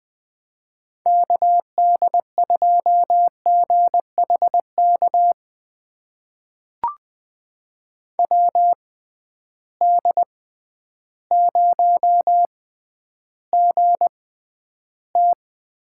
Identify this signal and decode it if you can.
Morse code, heard as a keyed tone, about 21 wpm, spaced as for 11 wpm, 700 Hz. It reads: KD2GHK WD0GT
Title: USA Call Signs - 6 Characters - 6x Character Spacing - 20wpm